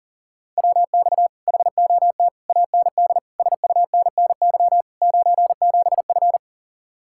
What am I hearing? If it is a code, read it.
Morse code, heard as a keyed tone, about 37 wpm, 700 Hz. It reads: WX HOT AND SUNNY 97F
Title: Speed-Racing Common QSO Phrases Encore - 40wpm